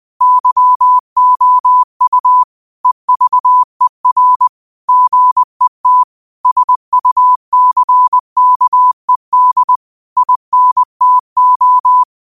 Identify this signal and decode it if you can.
Morse code, heard as a keyed tone, about 20 wpm, 1000 Hz. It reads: YOU EVER GET SUCKED INTO
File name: qotdfiles / 20wpm